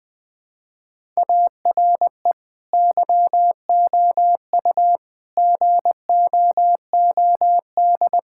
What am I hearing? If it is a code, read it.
Morse code, heard as a keyed tone, about 20 wpm, 700 Hz. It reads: ARE YOU GOOD
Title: Sentences from Top 1000 Words - Review Part 6 - 20wpm